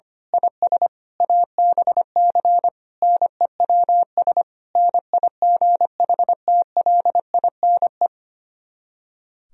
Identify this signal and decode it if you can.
Morse code, heard as a keyed tone, about 24 wpm, 700 Hz. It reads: IS ABC NEWS NIGHTLINE
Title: Speed-Racing - Sets of 5 Words 25wpm